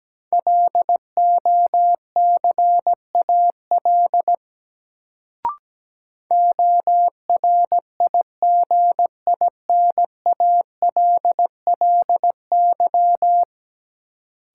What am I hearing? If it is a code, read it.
Morse code, heard as a keyed tone, about 17 wpm, 700 Hz. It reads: LOCAL ORIGINALLY
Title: Sentences from Top 1500 Words - New Word Drill Alt - 17wpm